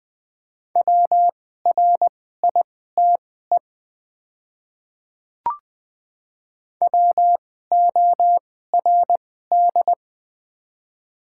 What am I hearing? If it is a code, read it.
Morse code, heard as a keyed tone, about 21 wpm, spaced as for 15 wpm, 700 Hz. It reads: WRITE WORD
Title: 2x Character Spacing - Top 200 Words - 20wpm